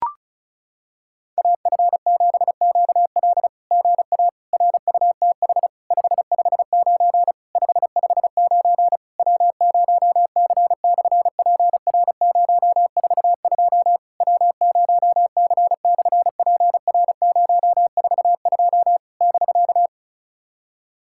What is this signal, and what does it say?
Morse code, heard as a keyed tone, about 36 wpm, 700 Hz. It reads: AF7QL GA RUTH 559 559 W0C/PR042 W0C/PR042 <BK>
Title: SOTA QSO Practice 35wpm